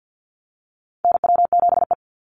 Morse code, heard as a keyed tone, about 50 wpm, 700 Hz. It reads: NW7E
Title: NA SOTA Chaser Callsigns 50wpm